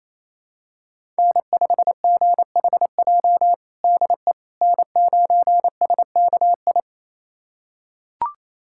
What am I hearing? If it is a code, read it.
Morse code, heard as a keyed tone, about 28 wpm, 690 Hz. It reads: N5GHJ DE N9SKI
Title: USA Callsign DE Callsign - 5 Characters - 28wpm